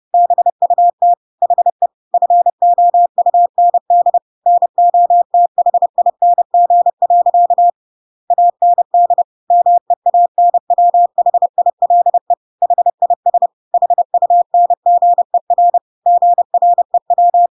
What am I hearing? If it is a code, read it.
Morse code, heard as a keyed tone, about 30 wpm, 700 Hz. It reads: BUT HE FOUND NOTHING. AND MEANWHILE HIS HUNGER GREW
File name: Pinocchio / Pinocchio ch5 30 30 ik1hge